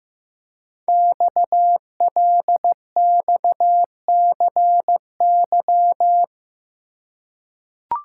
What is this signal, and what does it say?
Morse code, heard as a keyed tone, about 15 wpm, 700 Hz. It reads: XLXCY
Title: Code Groups - 5 Letters - 15wpm